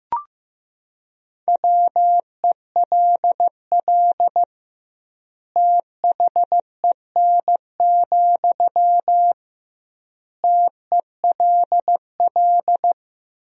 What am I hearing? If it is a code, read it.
Morse code, heard as a keyed tone, about 15 wpm, 700 Hz. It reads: WELL THEN, TELL
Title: 2x Word-Spacing - Sentences from Top 200 Words - 15wpm